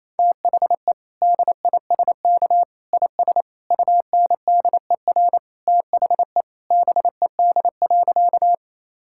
Morse code, heard as a keyed tone, about 28 wpm, 700 Hz. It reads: THE DISK IS UNDER THE BED.
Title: Sentences from Top 1000 Words - Review Part 6 - 28wpm